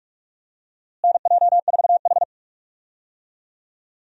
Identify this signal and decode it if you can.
Morse code, heard as a keyed tone, about 46 wpm, 700 Hz. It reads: NJ4H
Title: USA Call Signs - 4 Characters - Repeated 2x - 45wpm